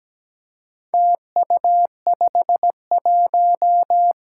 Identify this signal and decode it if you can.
Morse code, heard as a keyed tone, about 17 wpm, 700 Hz. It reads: TU51
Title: Four Challenging Letter-Number Combinations 17wpm